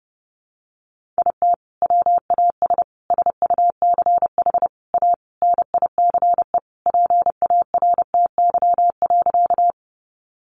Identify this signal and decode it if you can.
Morse code, heard as a keyed tone, about 30 wpm, 700 Hz. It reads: IT WAS SUCH A NICE PARTY.
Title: Sentences from Top 1400 Words 30wpm